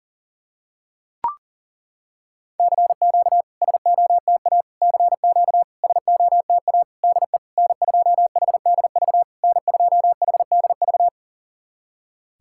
Morse code, heard as a keyed tone, about 40 wpm, 690 Hz. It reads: CQ SOTA CQ SOTA DE N2HDV N2HDV
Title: SOTA QSO Practice 40wpm